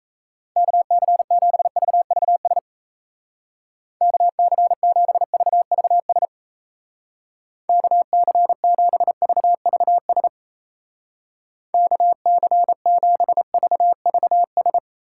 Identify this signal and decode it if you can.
Morse code, heard as a keyed tone, about 36 wpm, 700 Hz. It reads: KC7VVS KC7VVS KC7VVS KC7VVS